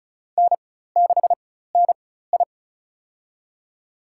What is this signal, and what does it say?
Morse code, heard as a keyed tone, about 34 wpm, spaced as for 16 wpm, 700 Hz. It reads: N6NI
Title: USA Call Signs - 4x Character Spacing - 35wpm